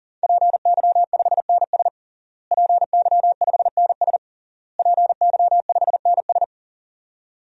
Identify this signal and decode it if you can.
Morse code, heard as a keyed tone, about 37 wpm, 700 Hz. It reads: PY5NS PY5NS PY5NS
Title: DXCC-100 Call Signs - 5 Characters - Repeated 3x - 40wpm